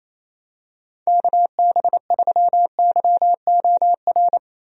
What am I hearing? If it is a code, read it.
Morse code, heard as a keyed tone, about 27 wpm, 700 Hz. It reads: KB3YOR
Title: USA Call Signs - 6 Characters - Repeated 2x - 28wpm